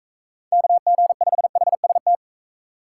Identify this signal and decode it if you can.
Morse code, heard as a keyed tone, about 44 wpm, 700 Hz. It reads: KG5HST